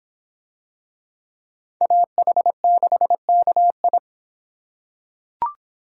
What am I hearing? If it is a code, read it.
Morse code, heard as a keyed tone, about 24 wpm, 700 Hz. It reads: AH6KI